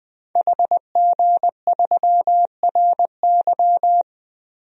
Morse code, heard as a keyed tone, about 21 wpm, 700 Hz. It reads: HG3RY
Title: DXCC-100 Call Signs - 5 Characters - Repeated 3x - 20wpm